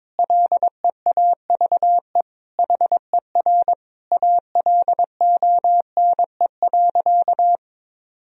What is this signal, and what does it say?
Morse code, heard as a keyed tone, about 22 wpm, 700 Hz. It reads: LEAVE HER ALONE.
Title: Sentences from Top 1000 Words - Review Part 4 - 22wpm